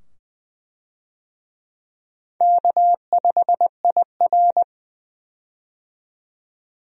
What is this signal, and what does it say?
Morse code, heard as a keyed tone, about 20 wpm, 700 Hz. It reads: K5IR
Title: USA Call Signs - 4 Characters - 20wpm